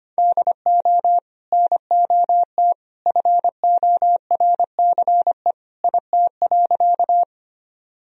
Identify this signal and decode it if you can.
Morse code, heard as a keyed tone, about 25 wpm, 700 Hz. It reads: DO NOT FORCE IT.
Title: Sentences from Top 1000 Words - Review Part 3 - 25wpm